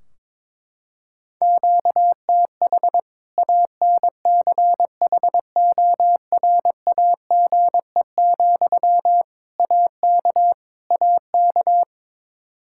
Morse code, heard as a keyed tone, about 21 wpm, 700 Hz. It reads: QTH ANCHORAGE, AK AK
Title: K1USN SST QSO Practice 22wpm